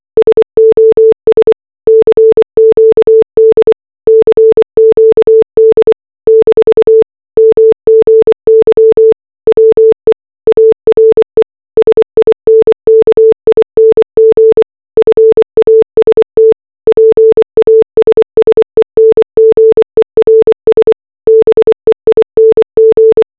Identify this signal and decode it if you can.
Morse code, heard as a keyed tone, about 24 wpm, 440 Hz. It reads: SOS CQD CQD - MGY WE ARE SINKING FAST PASSENGERS BEING
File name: RMS Titanic distress signal simulated as morse code